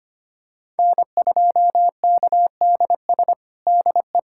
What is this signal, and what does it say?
Morse code, heard as a keyed tone, about 25 wpm, 700 Hz. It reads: N2KDS DE
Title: USA Callsign DE Callsign - 5 Characters - 25wpm